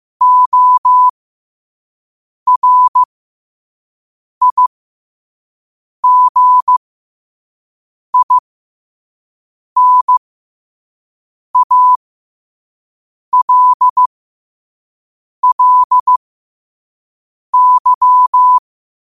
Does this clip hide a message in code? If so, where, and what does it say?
Yes: Morse code from 0.2 to 18.6 s, heard as a keyed tone, about 15 wpm, spaced as for 5 wpm, 1000 Hz. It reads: ORIGINALLY